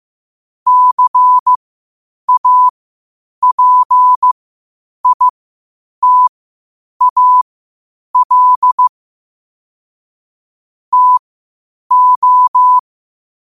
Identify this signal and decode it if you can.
Morse code, heard as a keyed tone, about 15 wpm, spaced as for 8 wpm, 1000 Hz. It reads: CAPITAL TO
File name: qotdfiles / 8wpm